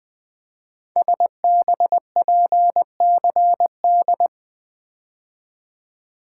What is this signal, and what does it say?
Morse code, heard as a keyed tone, about 20 wpm, 700 Hz. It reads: SBPCD
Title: Code Groups - 5 Letters - 20wpm